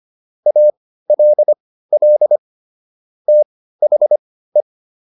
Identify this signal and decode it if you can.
Morse code, heard as a keyed tone, about 25 wpm, spaced as for 15 wpm, 600 Hz. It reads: ALL THE